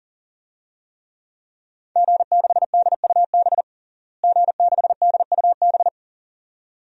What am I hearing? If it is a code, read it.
Morse code, heard as a keyed tone, about 37 wpm, 700 Hz. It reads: G6DUB G6DUB
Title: DXCC-100 Call Signs - 5 Characters - Repeated 2x - 40wpm